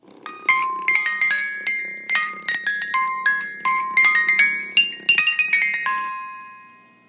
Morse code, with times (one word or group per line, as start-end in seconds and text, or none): none